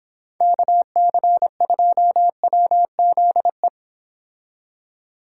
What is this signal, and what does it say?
Morse code, heard as a keyed tone, about 26 wpm, 700 Hz. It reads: KC2WZE